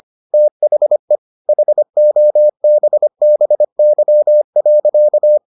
Morse code, heard as a keyed tone, about 25 wpm, 600 Hz. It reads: THE HOBBY.